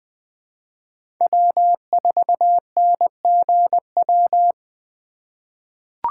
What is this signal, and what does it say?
Morse code, heard as a keyed tone, about 20 wpm, 700 Hz. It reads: W4NGW
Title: USA Call Signs - 5 Characters - 2x Character Spacing - 20wpm